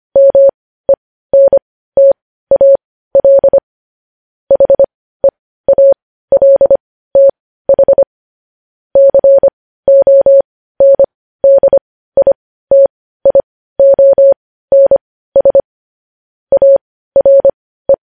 Morse code, heard as a keyed tone, about 25 wpm, spaced as for 15 wpm, 550 Hz. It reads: MENTAL HEALTH CONDITIONS ARE
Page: news-headlines-in-morse-code-at-15-wpm